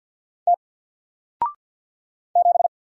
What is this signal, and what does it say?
Morse code, heard as a keyed tone, about 50 wpm, 700 Hz. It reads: T 7
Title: Single Letter-Number Alt 50wpm